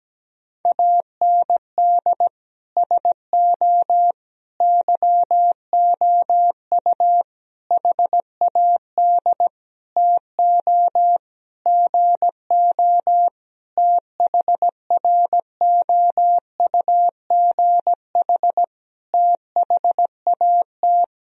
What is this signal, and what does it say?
Morse code, heard as a keyed tone, about 17 wpm, 700 Hz. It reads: AND SO YOU HAD TO GO THROUGH THAT
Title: Sentences from Top 1000 Words - Review Part 1 - 17wpm